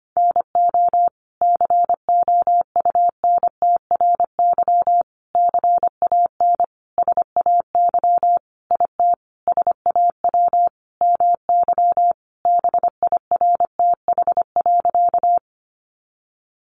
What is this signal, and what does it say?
Morse code, heard as a keyed tone, about 25 wpm, 700 Hz. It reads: NO COUNTRY CAN SAY IT SAW MY BIRTH.
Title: Sentences from Top 800 Words 25wpm